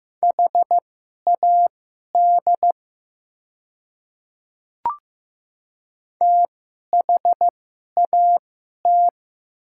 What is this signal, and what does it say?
Morse code, heard as a keyed tone, about 15 wpm, spaced as for 11 wpm, 700 Hz. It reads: HAD THAT
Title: 2x Character Spacing - Top 200 Words - 15wpm